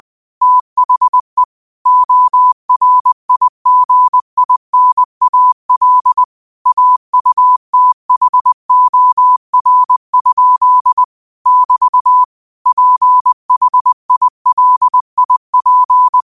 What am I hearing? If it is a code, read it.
Morse code, heard as a keyed tone, about 20 wpm, 1000 Hz. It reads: THE ORIGINAL AUTHOR? <BT> PHILIP
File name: qotdfiles / 20wpm